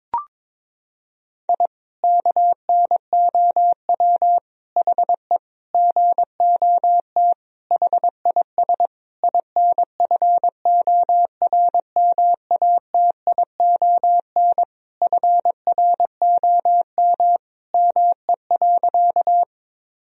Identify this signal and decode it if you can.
Morse code, heard as a keyed tone, about 22 wpm, 700 Hz. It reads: I KNOW HE GOT HIS INFORMATION FROM ME.
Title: Sentences from Top 1000 Words - Review Part 3 - 22wpm